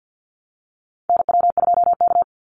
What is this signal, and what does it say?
Morse code, heard as a keyed tone, about 50 wpm, 700 Hz. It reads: NW2K
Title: Call Signs Repeated 3x - 50wpm